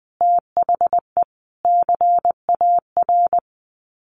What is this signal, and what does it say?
Morse code, heard as a keyed tone, about 20 wpm, 700 Hz. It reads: THE CAR
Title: Sets of 5 Words Encore 20wpm